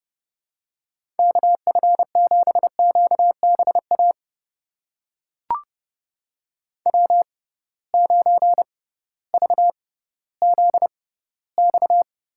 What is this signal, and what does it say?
Morse code, heard as a keyed tone, about 30 wpm, spaced as for 16 wpm, 700 Hz. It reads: KF7QBA W9VZX
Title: USA Call Signs - 6x Character Spacing - 30wpm